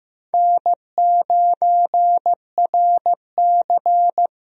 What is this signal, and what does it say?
Morse code, heard as a keyed tone, about 15 wpm, 700 Hz. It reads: N9RC
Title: USA Call Signs - 4 Characters - Repeated 3x - 15wpm